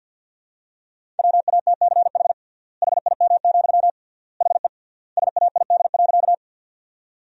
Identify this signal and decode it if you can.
Morse code, heard as a keyed tone, about 47 wpm, 700 Hz. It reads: WATCH HIM, HE SAID.
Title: 3x Word-Spacing - Sentences from Top 300 Words - 50wpm